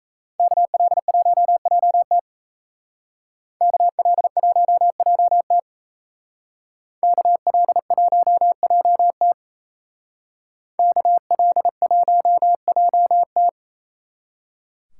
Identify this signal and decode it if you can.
Morse code, heard as a keyed tone, about 36 wpm, 700 Hz. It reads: KL1JT KL1JT KL1JT KL1JT